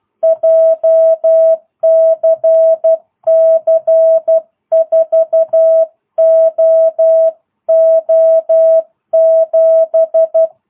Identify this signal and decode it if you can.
Morse code, heard as a keyed tone, about 12 wpm, 650 Hz. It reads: JCC 4OO7